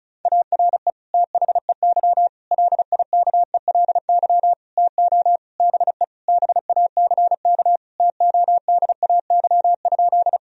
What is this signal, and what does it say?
Morse code, heard as a keyed tone, about 34 wpm, 700 Hz. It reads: ARE THEY LIKELY TO BE BACK TODAY?
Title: Sentences from Top 1000 Words - Review Part 4 - 35wpm